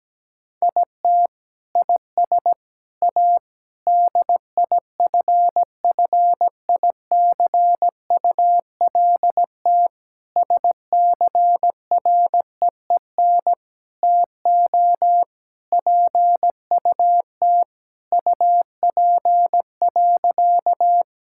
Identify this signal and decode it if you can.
Morse code, heard as a keyed tone, about 17 wpm, 700 Hz. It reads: IT IS A DIFFICULT SCREEN TO PUT UP.
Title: Sentences from Top 1000 Words - Review Part 6 - 17wpm